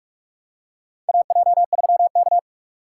Morse code, heard as a keyed tone, about 42 wpm, 690 Hz. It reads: AJ3K
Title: USA Call Signs - 4 Characters - 4x Character Spacing - 45wpm